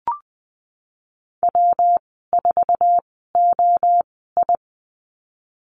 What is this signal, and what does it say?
Morse code, heard as a keyed tone, about 20 wpm, spaced as for 15 wpm, 700 Hz. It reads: W4OI
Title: 186 Lesson 30g - Letter b - Call Signs 20wpm